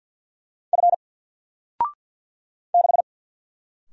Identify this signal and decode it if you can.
Morse code, heard as a keyed tone, about 50 wpm, 700 Hz. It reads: F 6
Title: Single Letter-Number 50wpm